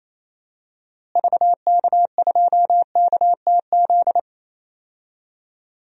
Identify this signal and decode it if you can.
Morse code, heard as a keyed tone, about 28 wpm, 700 Hz. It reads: VK2KTZ
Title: DXCC-100 Call Signs - 6 Characters - Repeated 2x - 28wpm